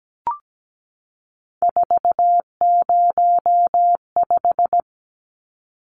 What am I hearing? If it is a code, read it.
Morse code, heard as a keyed tone, about 17 wpm, 700 Hz. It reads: 405
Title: Numbers - 3-Digit 17wpm